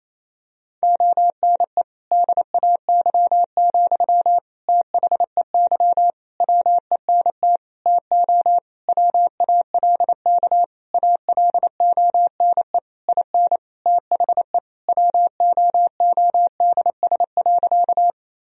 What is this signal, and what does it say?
Morse code, heard as a keyed tone, about 28 wpm, 700 Hz. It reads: ONE DAY, THEY WENT TO WALK ALONE IN THE WOODS.
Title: Sentences from Top 1000 Words - Review Part 2 - 28wpm